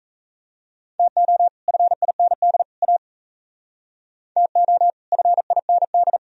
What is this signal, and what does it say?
Morse code, heard as a keyed tone, about 35 wpm, 700 Hz. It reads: TO FIND A TO FIND
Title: Speed-Racing - Sets of 3 Words 28wpm